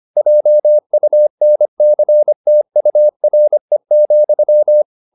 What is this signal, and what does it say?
Morse code, heard as a keyed tone, about 25 wpm, 600 Hz. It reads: JUNCTURE,